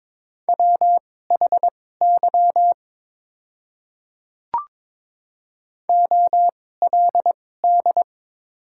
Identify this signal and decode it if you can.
Morse code, heard as a keyed tone, about 21 wpm, spaced as for 16 wpm, 700 Hz. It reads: WHY OLD
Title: 2x Character Spacing - Top 200 Words - 22wpm